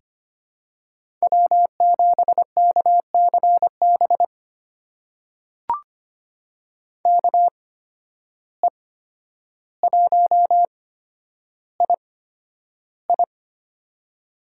Morse code, heard as a keyed tone, about 25 wpm, spaced as for 11 wpm, 700 Hz. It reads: W7KCB KE1II
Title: USA Call Signs - 5 Characters - 8x Character Spacing - 25wpm